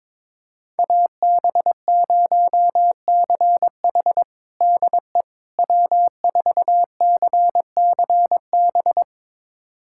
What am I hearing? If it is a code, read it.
Morse code, heard as a keyed tone, about 22 wpm, 700 Hz. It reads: AB0CH DE W4CCB
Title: USA Callsign DE Callsign - 5 Characters - 22wpm